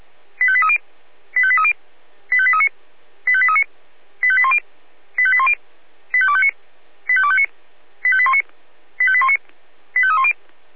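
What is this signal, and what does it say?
Five-tone selective call (ZVEI), heard as quick runs of five tones, tones of about 70 ms. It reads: 85830 85830 85839 85839 85810 85810 85259 85259 85819 85819 85310
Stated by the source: ZVEI five-tone selective call received by SDR.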